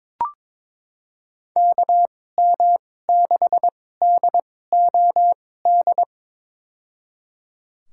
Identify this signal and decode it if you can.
Morse code, heard as a keyed tone, about 23 wpm, spaced as for 16 wpm, 700 Hz. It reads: KM6DOD
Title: USA Call Signs - 2x Character Spacing - 22wpm